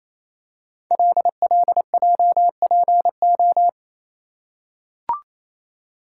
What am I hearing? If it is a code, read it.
Morse code, heard as a keyed tone, about 29 wpm, 690 Hz. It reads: LLJPO
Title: Code Groups - 5 Letters - 28wpm